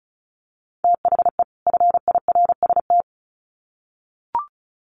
Morse code, heard as a keyed tone, about 34 wpm, 700 Hz. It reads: THE FIRST
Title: Sets of 2 Words 35wpm